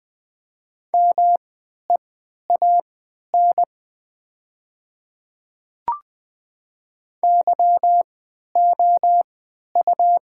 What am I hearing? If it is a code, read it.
Morse code, heard as a keyed tone, about 20 wpm, spaced as for 11 wpm, 700 Hz. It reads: MEAN YOU
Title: 3x Character Spacing - Top 200 Words - 20wpm